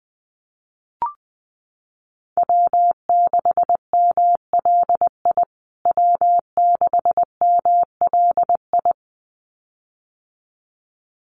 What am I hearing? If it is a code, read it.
Morse code, heard as a keyed tone, about 20 wpm, 700 Hz. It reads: W6MLI W6MLI
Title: USA Call Signs Repeated 2x 20wpm